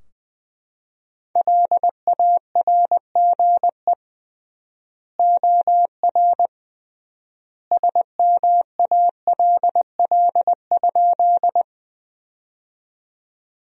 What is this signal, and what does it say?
Morse code, heard as a keyed tone, about 20 wpm, 700 Hz. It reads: LARGE OR SMALL?
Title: 3x Word-Spacing - Sentences from Top 200 Words - 20wpm